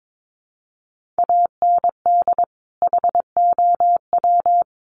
Morse code, heard as a keyed tone, about 22 wpm, 700 Hz. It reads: AND HOW
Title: Sentences from Top 1500 Words 22wpm